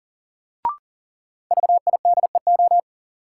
Morse code, heard as a keyed tone, about 40 wpm, 700 Hz. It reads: VIDEO
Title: Sentences from Top 1000 Words - Review Part 4 - New Word Drill - 40wpm